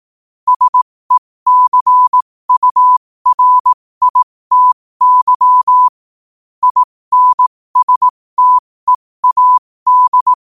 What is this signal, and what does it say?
Morse code, heard as a keyed tone, about 18 wpm, spaced as for 15 wpm, 1000 Hz. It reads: SECURITY INSTEAD
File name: qotdfiles / 15wpm